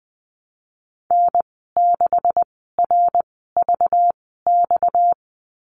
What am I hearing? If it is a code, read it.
Morse code, heard as a keyed tone, about 20 wpm, 700 Hz. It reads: N 6 R V X
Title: 231 Lesson 37g - Letter x - Call Signs 20wpm